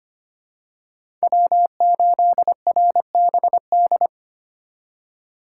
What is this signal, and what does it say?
Morse code, heard as a keyed tone, about 25 wpm, 700 Hz. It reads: W8RBD
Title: USA Call Signs - 3x Character Spacing - 25wpm